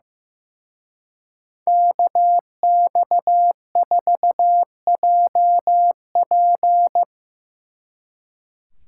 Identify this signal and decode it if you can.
Morse code, heard as a keyed tone, about 16 wpm, 700 Hz. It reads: KX4JP
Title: Speed Racing - USA Call Signs 15wpm